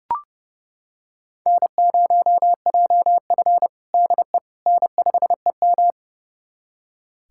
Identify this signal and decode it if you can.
Morse code, heard as a keyed tone, about 32 wpm, 700 Hz. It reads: N0JF DE N5EM
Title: USA Callsign DE Callsign - 4 Characters - 30wpm